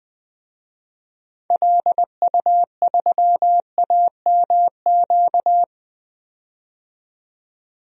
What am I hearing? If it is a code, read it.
Morse code, heard as a keyed tone, about 20 wpm, 700 Hz. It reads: LU3AMQ
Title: DXCC-100 Call Signs - 6 Characters - 20wpm